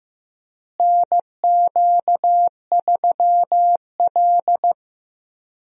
Morse code, heard as a keyed tone, about 15 wpm, 700 Hz. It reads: NQ3L
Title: USA Call Signs - 4 Characters - Repeated 2x - 15wpm